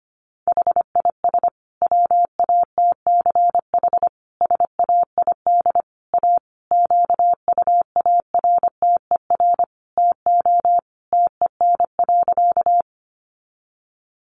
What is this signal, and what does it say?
Morse code, heard as a keyed tone, about 25 wpm, 700 Hz. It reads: HIS WATCH SAID A QUARTER TO TEN.
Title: Sentences from Top 1300 Words 25wpm